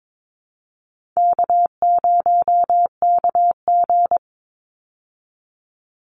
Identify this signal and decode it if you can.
Morse code, heard as a keyed tone, about 22 wpm, 700 Hz. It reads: K0KG
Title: Easy Call Signs Repeated 2x - 22wpm